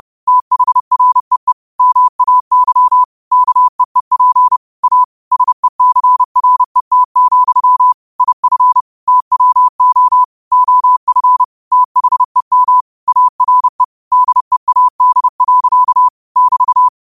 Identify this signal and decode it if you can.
Morse code, heard as a keyed tone, about 30 wpm, 1000 Hz. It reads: THREE MAY KEEP A SECRET, IF TWO OF THEM ARE DEAD. <BT>
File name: qotdfiles / 30wpm